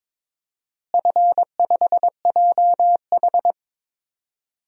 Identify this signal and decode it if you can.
Morse code, heard as a keyed tone, about 21 wpm, 700 Hz. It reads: F5JH
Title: DXCC-100 Call Signs - 4 Characters - Repeated 2x - 22wpm